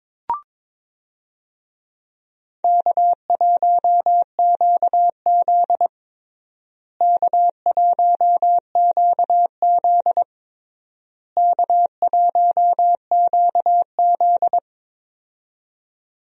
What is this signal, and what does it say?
Morse code, heard as a keyed tone, about 22 wpm, 700 Hz. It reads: K1QZ K1QZ K1QZ
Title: USA Call Signs - 4 Characters - Repeated 3x - 22wpm